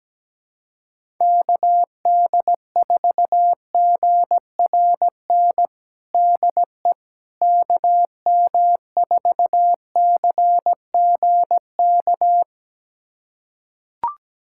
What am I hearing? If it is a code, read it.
Morse code, heard as a keyed tone, about 17 wpm, 700 Hz. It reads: KD4GRN DE KM4CGK
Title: USA Callsign DE Callsign - 6 Characters - 17wpm